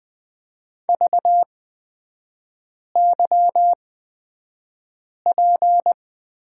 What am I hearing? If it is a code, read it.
Morse code, heard as a keyed tone, about 20 wpm, spaced as for 5 wpm, 700 Hz. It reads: VYP
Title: 224 Lesson 36e - BT - Three Characters 20wpm